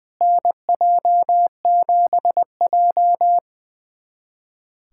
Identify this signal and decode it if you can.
Morse code, heard as a keyed tone, about 20 wpm, 700 Hz. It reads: NJ7J
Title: USA Callsign DE Callsign - 4 Characters - 20wpm